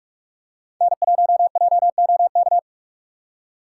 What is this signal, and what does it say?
Morse code, heard as a keyed tone, about 46 wpm, 700 Hz. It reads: N1JOK
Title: USA Call Signs - 5 Characters - 6x Character Spacing - 45wpm